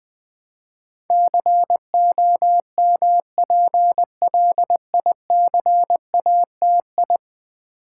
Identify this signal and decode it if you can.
Morse code, heard as a keyed tone, about 20 wpm, 700 Hz. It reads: COMPLICATI
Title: Build A Word - 5 Letters At A Time - 12-Letter Words - 20wpm